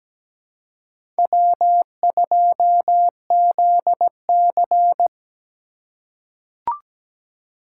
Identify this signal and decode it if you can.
Morse code, heard as a keyed tone, about 17 wpm, 700 Hz. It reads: W2ZC
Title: USA Call Signs - 3x Character Spacing - 17wpm